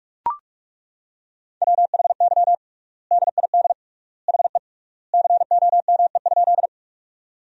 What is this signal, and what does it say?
Morse code, heard as a keyed tone, about 41 wpm, 700 Hz. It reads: WHY DID HE COME?
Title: 3x Word-Spacing - Sentences from Top 200 Words - 45wpm